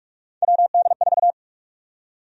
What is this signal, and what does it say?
Morse code, heard as a keyed tone, about 43 wpm, 700 Hz. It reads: WD4
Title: USA Call Sign Prefixes - Repeated 2x - 45wpm